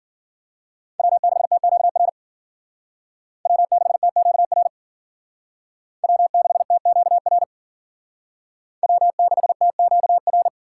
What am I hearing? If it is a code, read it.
Morse code, heard as a keyed tone, about 51 wpm, 700 Hz. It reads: W6TQR W6TQR W6TQR W6TQR